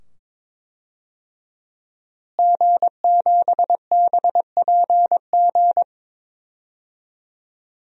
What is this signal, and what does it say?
Morse code, heard as a keyed tone, about 22 wpm, 700 Hz. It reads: G7BPG
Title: DXCC-100 Call Signs - 5 Characters - Repeated 2x - 22wpm